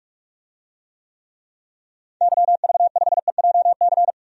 Morse code, heard as a keyed tone, about 46 wpm, 700 Hz. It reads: YV5EJC
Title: DXCC-100 Call Signs - 6 Characters - 45wpm